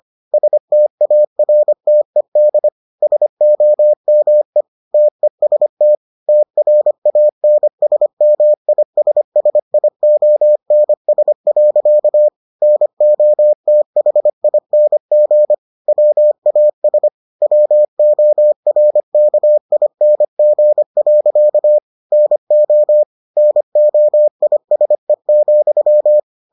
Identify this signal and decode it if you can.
Morse code, heard as a keyed tone, about 25 wpm, 600 Hz. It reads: STARTED SOME TEST TRANSMISSIONS. NOTHING WAS WORKING. NO NOISE,